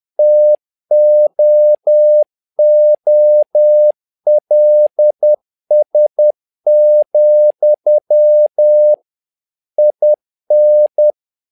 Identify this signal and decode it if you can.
Morse code, heard as a keyed tone, about 10 wpm, 600 Hz. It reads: TOOLS, IN
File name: learning center / tower trouble / tower trouble 10wpm